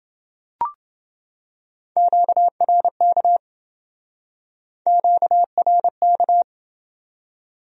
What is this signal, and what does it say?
Morse code, heard as a keyed tone, about 28 wpm, 700 Hz. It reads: QRK QRK